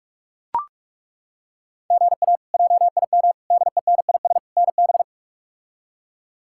Morse code, heard as a keyed tone, about 42 wpm, 700 Hz. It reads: GA JIM DENIS NB
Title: K1USN SST QSO Practice - 45wpm